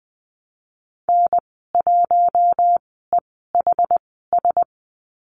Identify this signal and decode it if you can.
Morse code, heard as a keyed tone, about 20 wpm, spaced as for 14 wpm, 700 Hz. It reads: N1EHS
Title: 062 Lesson 11g - Letter h - Call Signs 20wpm